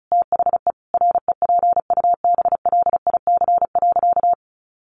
Morse code, heard as a keyed tone, about 34 wpm, 700 Hz. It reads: THE REPUBLIC.
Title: Sentences from Top 1500 Words 35wpm